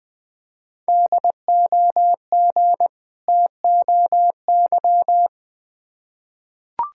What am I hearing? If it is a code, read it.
Morse code, heard as a keyed tone, about 20 wpm, 690 Hz. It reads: DOG TOY
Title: Difficult Words with Hints - 20wpm